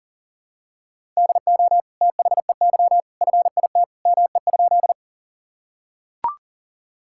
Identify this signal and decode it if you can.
Morse code, heard as a keyed tone, about 40 wpm, 700 Hz. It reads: DO THEY FIT ME?
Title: Sentences from Top 1000 Words - Review Part 5 - 40wpm